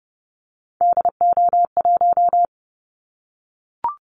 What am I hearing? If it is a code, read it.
Morse code, heard as a keyed tone, about 30 wpm, 700 Hz. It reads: DO1
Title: Callsign Prefixes for Top 25 DXCC - 30wpm